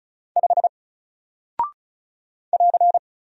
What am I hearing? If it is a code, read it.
Morse code, heard as a keyed tone, about 35 wpm, 700 Hz. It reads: 5 <AR>
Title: Single Character (Letter:Number:Punctuation:Prosign) Alt 35wpm